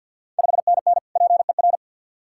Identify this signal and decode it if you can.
Morse code, heard as a keyed tone, about 50 wpm, 690 Hz. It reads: 5NN PER
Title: Ontario QSO Party Exchange 50wpm